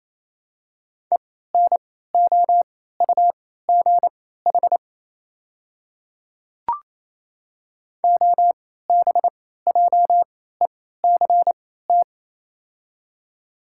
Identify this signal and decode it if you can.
Morse code, heard as a keyed tone, about 28 wpm, spaced as for 16 wpm, 690 Hz. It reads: ENOUGH OBJECT
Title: 3x Character Spacing - Top 500 Words - 28wpm